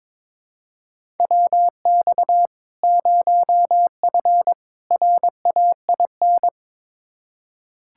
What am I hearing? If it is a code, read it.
Morse code, heard as a keyed tone, about 22 wpm, 700 Hz. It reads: WX 0F RAIN
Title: Practice QSO - Exchange 2 - Info - 22wpm